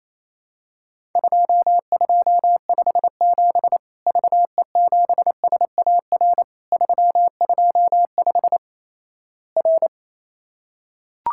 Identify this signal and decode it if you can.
Morse code, heard as a keyed tone, about 28 wpm, 700 Hz. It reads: 2257 VE7SAR 325 R
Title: Worked All Europe DX Contest - QTC Exchange - 28wpm